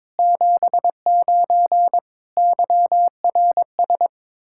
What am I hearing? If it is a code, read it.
Morse code, heard as a keyed tone, about 22 wpm, 700 Hz. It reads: 79 YRS
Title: Practice QSO - Exchange 3 - More Info - 22wpm